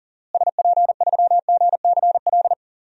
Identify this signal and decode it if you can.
Morse code, heard as a keyed tone, about 37 wpm, 700 Hz. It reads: SP3GCL
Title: DXCC-100 Call Signs - 6 Characters - Repeated 2x - 40wpm